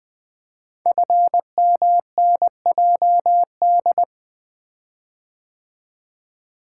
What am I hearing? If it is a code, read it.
Morse code, heard as a keyed tone, about 20 wpm, 700 Hz. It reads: FMNJD
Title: Code Groups - 5 Letters - 20wpm